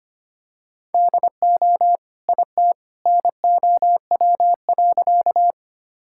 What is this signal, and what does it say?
Morse code, heard as a keyed tone, about 25 wpm, 700 Hz. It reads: DO IT NOW.
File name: practice / 25wpm / Sentences with 2-Word Phrases - 25wpm